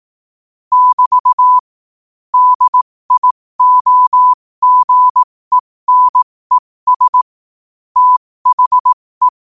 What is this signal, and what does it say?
Morse code, heard as a keyed tone, about 18 wpm, spaced as for 15 wpm, 1000 Hz. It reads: <BT> DIOGENES THE
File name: qotdfiles / 15wpm